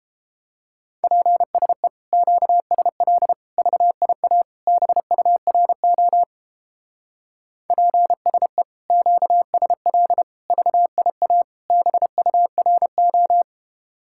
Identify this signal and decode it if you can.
Morse code, heard as a keyed tone, about 29 wpm, 700 Hz. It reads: PSE QSL VIA BURO PSE QSL VIA BURO